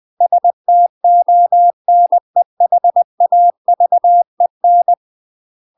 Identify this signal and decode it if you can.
Morse code, heard as a keyed tone, about 20 wpm, 700 Hz. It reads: STONEHAVEN
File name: UK MASTER QSOs TRAINING / 20WPM / UK MASTER QSOs 0052